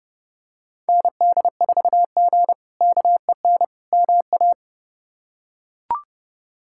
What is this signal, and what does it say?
Morse code, heard as a keyed tone, about 29 wpm, 700 Hz. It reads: ND4G KEN MA
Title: K1USN SST QSO Practice - 30wpm